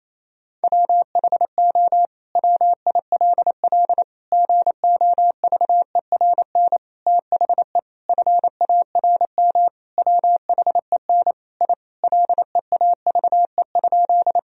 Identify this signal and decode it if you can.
Morse code, heard as a keyed tone, about 28 wpm, 700 Hz. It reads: WHO WILL GOVERN THE FARM WHEN I LEAVE?
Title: Sentences from Top 1000 Words - Review Part 2 - 28wpm